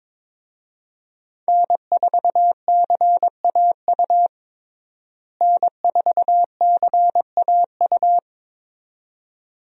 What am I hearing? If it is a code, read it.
Morse code, heard as a keyed tone, about 22 wpm, 700 Hz. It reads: N4CAU N4CAU
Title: USA Call Signs - 5 Characters - Repeated 2x - 22wpm